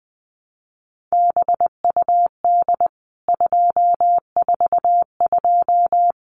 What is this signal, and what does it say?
Morse code, heard as a keyed tone, about 19 wpm, 700 Hz. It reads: BUD 242
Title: CWT Exchange 20wpm